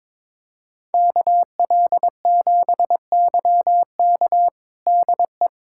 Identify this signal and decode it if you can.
Morse code, heard as a keyed tone, about 22 wpm, 700 Hz. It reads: KL7YK DE
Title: USA Callsign DE Callsign 22wpm